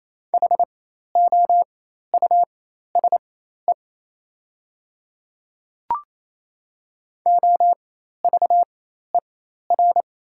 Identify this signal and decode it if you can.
Morse code, heard as a keyed tone, about 28 wpm, spaced as for 13 wpm, 700 Hz. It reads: HOUSE OVER
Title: 4x Character Spacing - Top 300 Words - 28wpm